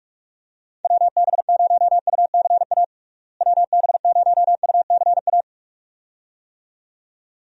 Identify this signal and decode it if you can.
Morse code, heard as a keyed tone, about 46 wpm, 700 Hz. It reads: WB0UCA WB0UCA
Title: USA Call Signs - 6 Characters - Repeated 2x - 45wpm